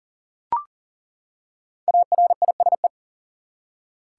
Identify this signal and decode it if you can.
Morse code, heard as a keyed tone, about 40 wpm, 700 Hz. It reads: ARISE
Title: Five Letter Words - 40wpm